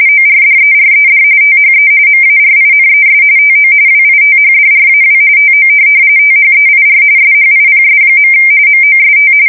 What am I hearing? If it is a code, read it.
RTTY, about 45.45 baud. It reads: ORDERED THE MARINES TO WITHDRAW TO THEIR SHIPS. MOST
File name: fsk170 / USS-New-Jersey